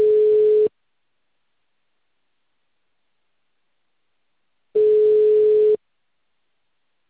Ringback tone (Europe, 425 Hz).